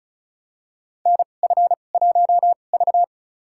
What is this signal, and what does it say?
Morse code, heard as a keyed tone, about 36 wpm, spaced as for 25 wpm, 700 Hz. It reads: NF1V
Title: USA Call Signs - 2x Character Spacing - 35wpm